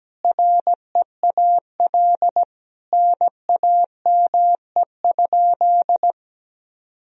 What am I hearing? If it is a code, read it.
Morse code, heard as a keyed tone, about 17 wpm, 700 Hz. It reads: REAL NAME?
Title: Sentences from Top 300 Words and Abbreviations 17wpm